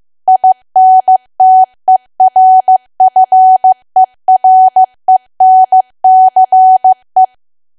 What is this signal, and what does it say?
Morse code, heard as a keyed tone, about 15 wpm, 750 Hz. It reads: INTERFERENCE
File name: cwa / W1AW-Level2 / 15WPM